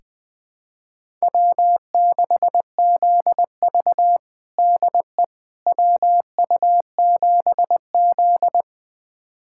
Morse code, heard as a keyed tone, about 21 wpm, 700 Hz. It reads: W6ZV DE WU7Z
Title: USA Callsign DE Callsign - 4 Characters - 20wpm